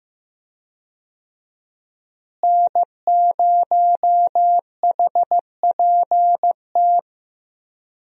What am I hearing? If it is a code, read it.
Morse code, heard as a keyed tone, about 16 wpm, 700 Hz. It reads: N0HPT
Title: USA Call Signs - 5 Characters - Repeated 3x - 15wpm